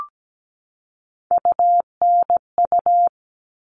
Morse code, heard as a keyed tone, about 17 wpm, 700 Hz. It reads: UNU
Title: All Cut Numbers - 3-Digit 17wpm